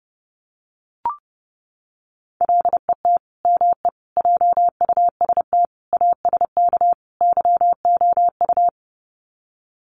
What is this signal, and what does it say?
Morse code, heard as a keyed tone, about 30 wpm, 700 Hz. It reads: LET ME JUST ASK YOU
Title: Sets of 5 Words Encore2 30wpm